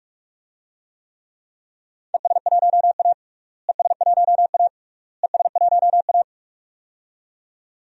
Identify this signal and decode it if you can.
Morse code, heard as a keyed tone, about 46 wpm, 700 Hz. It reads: ES1A ES1A ES1A
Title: DXCC-100 Call Signs - 4 Characters - Repeated 3x - 45wpm